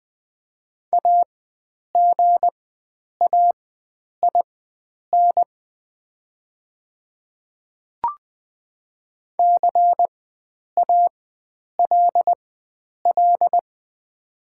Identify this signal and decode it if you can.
Morse code, heard as a keyed tone, about 20 wpm, spaced as for 9 wpm, 700 Hz. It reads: AGAIN CALL
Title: 4x Character Spacing - Top 500 Words - 20wpm